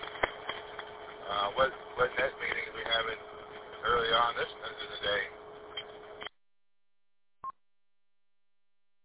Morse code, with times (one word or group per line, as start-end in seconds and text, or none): none